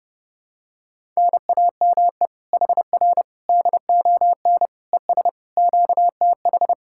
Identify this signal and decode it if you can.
Morse code, heard as a keyed tone, about 30 wpm, 700 Hz. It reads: NAME HR DON ES QTH